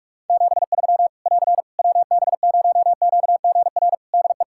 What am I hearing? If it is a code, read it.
Morse code, heard as a keyed tone, about 45 wpm, 700 Hz. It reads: 73 <AR> WB0QGR DE
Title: Practice QSO - Exchange 4 - Ending - 45wpm